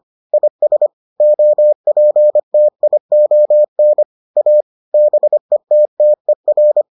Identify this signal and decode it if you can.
Morse code, heard as a keyed tone, about 25 wpm, 600 Hz. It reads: IS OPTION A BETTER